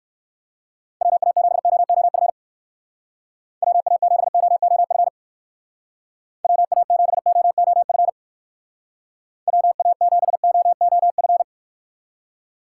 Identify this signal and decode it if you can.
Morse code, heard as a keyed tone, about 59 wpm, 700 Hz. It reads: WA7OOF WA7OOF WA7OOF WA7OOF